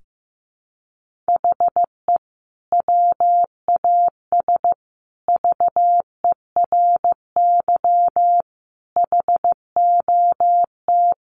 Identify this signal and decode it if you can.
Morse code, heard as a keyed tone, about 15 wpm, 700 Hz. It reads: HE WAS VERY HOT
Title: Sentences from Top 1400 Words 15wpm